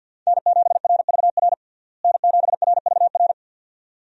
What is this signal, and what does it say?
Morse code, heard as a keyed tone, about 50 wpm, 700 Hz. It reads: N7RVR N7RVR
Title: USA Call Signs - 5 Characters - Repeated 2x - 50wpm